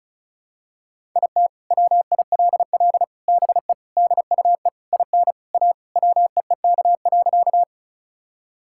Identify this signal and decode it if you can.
Morse code, heard as a keyed tone, about 36 wpm, 700 Hz. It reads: IT WILL BE DUE IN A WEEK.
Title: Sentences from Top 1000 Words - Review Part 6 - 35wpm